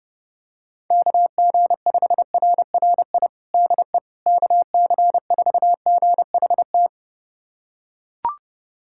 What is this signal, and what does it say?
Morse code, heard as a keyed tone, about 32 wpm, 700 Hz. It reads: KG5RRI DE KC4GHT
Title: USA Callsign DE Callsign - 6 Characters - 30wpm